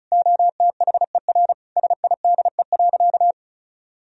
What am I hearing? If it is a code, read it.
Morse code, heard as a keyed tone, about 34 wpm, 700 Hz. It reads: OTHER SIDE.
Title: Sentences from Top 1000 Words - Review Part 1 - 35wpm